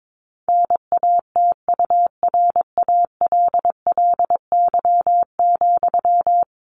Morse code, heard as a keyed tone, about 22 wpm, 700 Hz. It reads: NATURALLY,
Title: Sentences from Top 1500 Words 22wpm